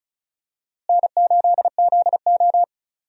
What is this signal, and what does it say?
Morse code, heard as a keyed tone, about 36 wpm, 700 Hz. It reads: N8ZO
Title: USA Call Signs - 4 Characters - Repeated 2x - 35wpm